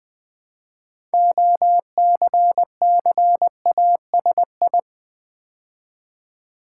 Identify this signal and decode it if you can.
Morse code, heard as a keyed tone, about 20 wpm, 700 Hz. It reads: OCCASI
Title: Build A Word - 2 Letters At A Time - 8-Letter Words - 20wpm